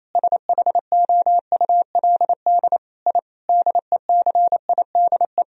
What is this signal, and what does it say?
Morse code, heard as a keyed tone, about 28 wpm, 700 Hz. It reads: SHOULD I DECIDE
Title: Sentences from Top 1000 Words - Review Part 3 - 28wpm